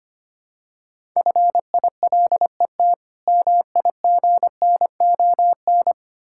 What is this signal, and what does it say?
Morse code, heard as a keyed tone, about 25 wpm, 700 Hz. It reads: FILET MIGNON
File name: practice / 25wpm / Difficult Words with Hints - 25wpm